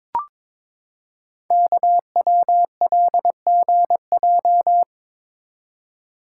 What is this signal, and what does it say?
Morse code, heard as a keyed tone, about 22 wpm, 700 Hz. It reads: KWLGJ
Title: Code Groups - 5 Letters - 22wpm